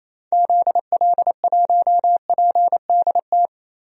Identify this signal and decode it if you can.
Morse code, heard as a keyed tone, about 28 wpm, 700 Hz. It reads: ZL1PDT
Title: DXCC-100 Call Signs - 6 Characters - 28wpm